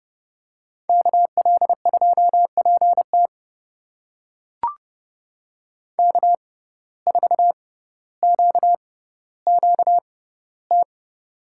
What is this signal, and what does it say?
Morse code, heard as a keyed tone, about 30 wpm, spaced as for 15 wpm, 700 Hz. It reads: KL2PT K4QQT
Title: USA Call Signs - 5 Characters - 6x Character Spacing - 30wpm